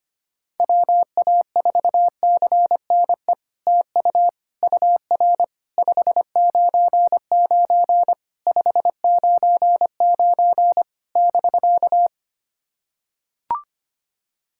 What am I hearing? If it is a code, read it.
Morse code, heard as a keyed tone, about 25 wpm, 700 Hz. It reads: WA4CNE TU UR 599 599 <BK>
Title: SOTA QSO Practice 25wpm